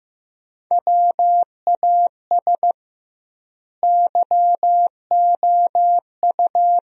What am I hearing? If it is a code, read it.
Morse code, heard as a keyed tone, about 15 wpm, 700 Hz. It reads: WAS YOU
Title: 2x Word-Spacing - Sentences from Top 300 Words - 15wpm